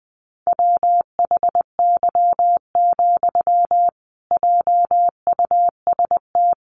Morse code, heard as a keyed tone, about 20 wpm, 700 Hz. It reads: WHY, JUST
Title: Sentences from Top 1000 Words 20wpm